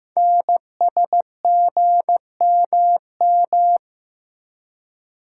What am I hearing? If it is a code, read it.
Morse code, heard as a keyed tone, about 15 wpm, 700 Hz. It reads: NSGMM
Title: Code Groups - 5 Letters - 15wpm